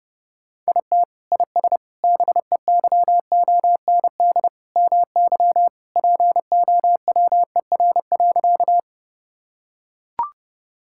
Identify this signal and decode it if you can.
Morse code, heard as a keyed tone, about 30 wpm, 700 Hz. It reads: IT IS BEYOND MY POWER.
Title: Sentences from Top 1000 Words - Review Part 6 - 30wpm